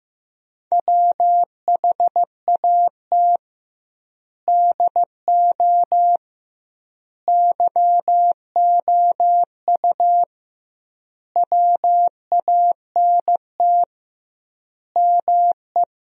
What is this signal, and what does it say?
Morse code, heard as a keyed tone, about 15 wpm, 700 Hz. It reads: WHAT DO YOU WANT ME
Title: 2x Word-Spacing - Sentences from Top 200 Words - 15wpm